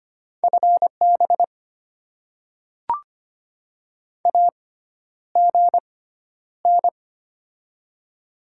Morse code, heard as a keyed tone, about 25 wpm, spaced as for 11 wpm, 700 Hz. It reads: FB AGN
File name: Top 20 QSO Elements - 6x Character Spacing - 25wpm